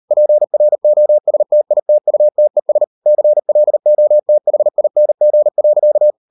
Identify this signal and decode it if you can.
Morse code, heard as a keyed tone, about 38 wpm, 600 Hz. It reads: PROSTITUTES CLOTHING.